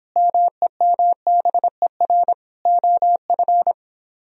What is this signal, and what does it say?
Morse code, heard as a keyed tone, about 24 wpm, 700 Hz. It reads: MEMBER OF